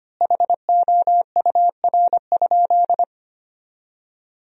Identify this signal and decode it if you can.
Morse code, heard as a keyed tone, about 25 wpm, 700 Hz. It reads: HOUR?
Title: 3x Word-Spacing - Sentences from Top 400 Words - 25wpm